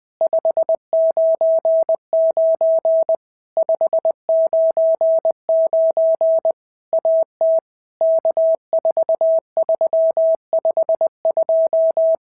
Morse code, heard as a keyed tone, about 19 wpm, 650 Hz. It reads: 599 599 AT K4352
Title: POTA QSO Practice - 20wpm